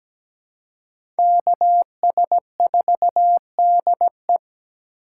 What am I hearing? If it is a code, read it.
Morse code, heard as a keyed tone, about 17 wpm, 700 Hz. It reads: KS4DE
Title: USA Call Signs - 5 Characters - 2x Character Spacing - 17wpm